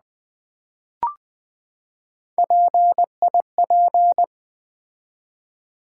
Morse code, heard as a keyed tone, about 20 wpm, 700 Hz. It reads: PIP
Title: Build A Word - 3 Letters At A Time - 8-Letter Words - 20wpm